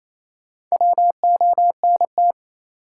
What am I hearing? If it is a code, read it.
Morse code, heard as a keyed tone, about 28 wpm, 700 Hz. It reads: WONT
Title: Sentences from Top 1000 Words - Review Part 6 - New Word Drill - 28wpm